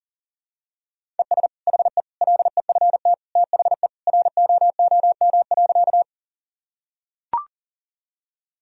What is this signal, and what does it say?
Morse code, heard as a keyed tone, about 40 wpm, 700 Hz. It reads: ES HE LEFT THE ROOM.
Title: Sentences from Top 300 Words and Abbreviations 40wpm